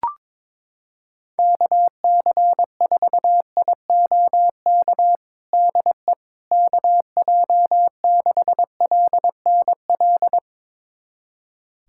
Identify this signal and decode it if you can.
Morse code, heard as a keyed tone, about 22 wpm, 700 Hz. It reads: KC4IOK DE KJ6LNL
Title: USA Callsign DE Callsign - 6 Characters - 22wpm